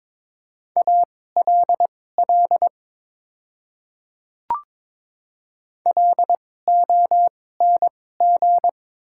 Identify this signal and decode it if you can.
Morse code, heard as a keyed tone, about 23 wpm, spaced as for 16 wpm, 700 Hz. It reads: ALL LONG
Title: 2x Character Spacing - Top 100 Words - 22wpm